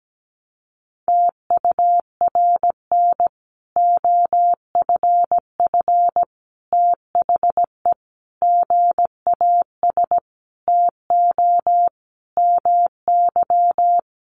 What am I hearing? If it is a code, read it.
Morse code, heard as a keyed tone, about 17 wpm, 700 Hz. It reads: TURN OFF THE GAS TO MY
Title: Sentences from Top 1100 Words 17wpm